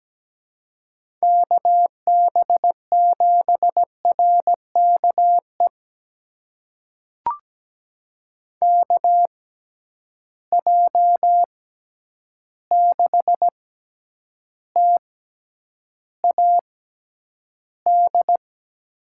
Morse code, heard as a keyed tone, about 17 wpm, spaced as for 9 wpm, 700 Hz. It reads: KB7RKE KJ6TAD
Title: USA Call Signs - 6 Characters - 6x Character Spacing - 17wpm